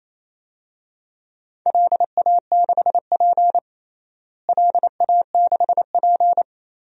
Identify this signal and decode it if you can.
Morse code, heard as a keyed tone, about 28 wpm, 700 Hz. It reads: LA6P LA6P
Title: DXCC-100 Call Signs - 4 Characters - Repeated 2x - 28wpm